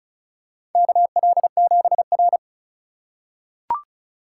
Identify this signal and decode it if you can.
Morse code, heard as a keyed tone, about 34 wpm, 700 Hz. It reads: KL7R
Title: USA Call Signs - 4 Characters - 2x Character Spacing - 35wpm